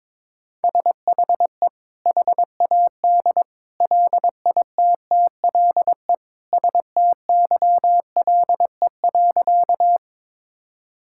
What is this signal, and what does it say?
Morse code, heard as a keyed tone, about 22 wpm, 700 Hz. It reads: SHE HAD LITTLE STYLE.
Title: Sentences from Top 1000 Words - Review Part 5 - 22wpm